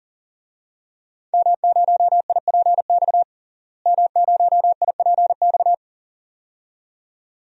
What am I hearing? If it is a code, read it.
Morse code, heard as a keyed tone, about 40 wpm, 700 Hz. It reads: M0IPX M0IPX
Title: DXCC-100 Call Signs - 5 Characters - Repeated 2x - 40wpm